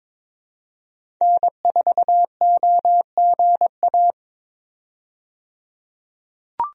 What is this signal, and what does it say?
Morse code, heard as a keyed tone, about 22 wpm, 700 Hz. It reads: N4OGA
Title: USA Call Signs - 5 Characters - 22wpm